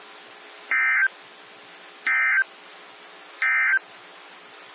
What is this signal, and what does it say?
Emergency Alert System (SAME) tones, heard as harsh buzzing bursts. It is heard from 0.6 to 3.9 s.
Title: NOAA Weather Radio - WXJ73